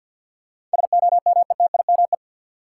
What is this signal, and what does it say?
Morse code, heard as a keyed tone, about 53 wpm, 700 Hz. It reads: SOMETIME
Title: Sentences from Top 1500 Words - New Word Drill Alt - 50wpm